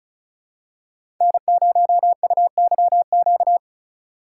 Morse code, heard as a keyed tone, about 36 wpm, 700 Hz. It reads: N0UYQ
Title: USA Call Signs - 5 Characters - 3x Character Spacing - 35wpm